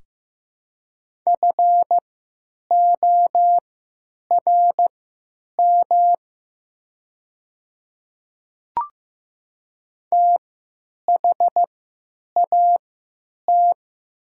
Morse code, heard as a keyed tone, about 15 wpm, spaced as for 8 wpm, 700 Hz. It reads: FORM THAT
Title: 3x Character Spacing - Top 300 Words - 15wpm